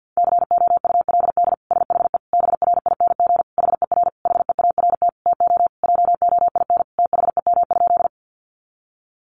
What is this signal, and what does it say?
Morse code, heard as a keyed tone, about 50 wpm, 700 Hz. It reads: COULD SHE BRING HER HEART TO JOIN THEM?
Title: Sentences from Top 1400 Words 50wpm